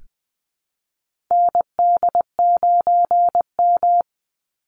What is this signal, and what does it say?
Morse code, heard as a keyed tone, about 20 wpm, 700 Hz. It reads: ND9M
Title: Call Signs Repeated 2x - 20wpm